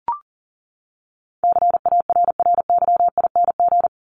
Morse code, heard as a keyed tone, about 40 wpm, 700 Hz. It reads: CARRYING
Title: ING Suffix - Top 100 Words with Base Words 40wpm